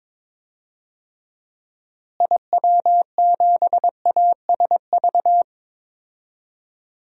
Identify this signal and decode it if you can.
Morse code, heard as a keyed tone, about 22 wpm, 700 Hz. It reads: IW7ASV
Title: DXCC-100 Call Signs - 6 Characters - Repeated 2x - 22wpm